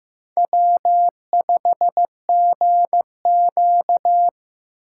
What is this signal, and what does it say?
Morse code, heard as a keyed tone, about 18 wpm, 700 Hz. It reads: W5GQ
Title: Speed Racing - USA 1x2 Call Signs 15wpm